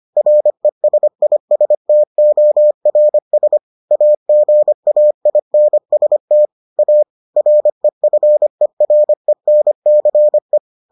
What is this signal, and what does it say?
Morse code, heard as a keyed tone, about 25 wpm, 600 Hz. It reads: RESISTORS AGAINST A REFERENCE